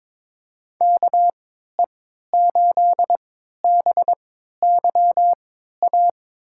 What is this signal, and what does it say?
Morse code, heard as a keyed tone, about 22 wpm, spaced as for 12 wpm, 700 Hz. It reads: KE8BYA
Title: USA Call Signs - 6 Characters - 3x Character Spacing - 22wpm